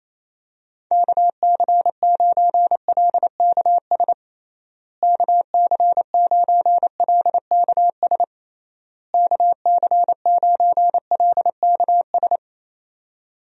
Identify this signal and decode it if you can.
Morse code, heard as a keyed tone, about 27 wpm, 700 Hz. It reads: KC9LKS KC9LKS KC9LKS
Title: USA Call Signs - 6 Characters - Repeated 3x - 28wpm